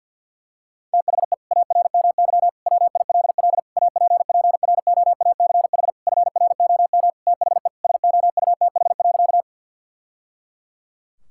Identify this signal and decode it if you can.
Morse code, heard as a keyed tone, about 46 wpm, 700 Hz. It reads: THE ARMY WILL APPROACH FROM THE SOUTH.
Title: Sentences from Top 1000 Words - Review Part 6 - 50wpm